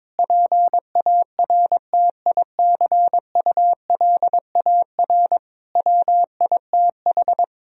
Morse code, heard as a keyed tone, about 22 wpm, 700 Hz. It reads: PARTICULAR WITH
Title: Sentences from Top 1000 Words - Review Part 4 - 22wpm